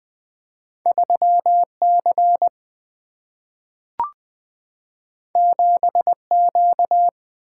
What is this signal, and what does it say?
Morse code, heard as a keyed tone, about 20 wpm, 700 Hz. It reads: 3 C 7 Q
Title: Single-Letter Single-Number 20wpm